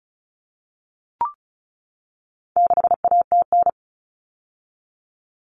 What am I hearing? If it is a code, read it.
Morse code, heard as a keyed tone, about 34 wpm, 700 Hz. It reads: 6ATN
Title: All Cut Numbers - 4-Digit 35wpm